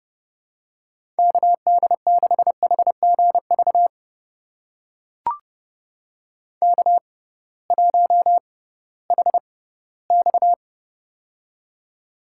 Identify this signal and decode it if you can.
Morse code, heard as a keyed tone, about 30 wpm, spaced as for 17 wpm, 700 Hz. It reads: KD6HGV K1HX
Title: USA Call Signs - 6x Character Spacing - 30wpm